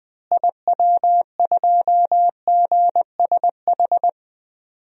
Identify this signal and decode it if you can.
Morse code, heard as a keyed tone, about 20 wpm, 700 Hz. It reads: IW2GSH
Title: DXCC-100 Call Signs - 6 Characters - Repeated 3x - 20wpm